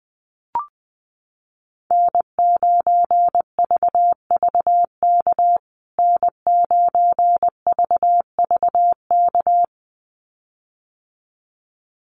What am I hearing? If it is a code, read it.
Morse code, heard as a keyed tone, about 20 wpm, 700 Hz. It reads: N9VVK N9VVK
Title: USA Call Signs Repeated 2x 20wpm